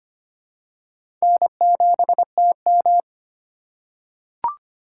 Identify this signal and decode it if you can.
Morse code, heard as a keyed tone, about 25 wpm, 700 Hz. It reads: N7TM
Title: USA Call Signs - 4 Characters - 2x Character Spacing - 25wpm